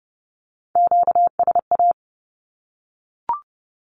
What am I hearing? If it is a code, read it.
Morse code, heard as a keyed tone, about 30 wpm, 700 Hz. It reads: QSA
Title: Q-Codes 30wpm